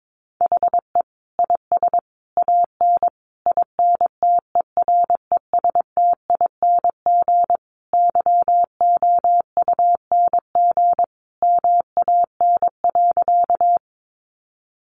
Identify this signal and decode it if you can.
Morse code, heard as a keyed tone, about 22 wpm, 700 Hz. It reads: HE IS AN INTERESTING YOUNG MAN.
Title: Sentences from Top 1000 Words 22wpm